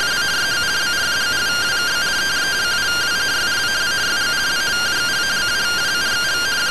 Two-tone teleprinter signal, about 170 Hz shift.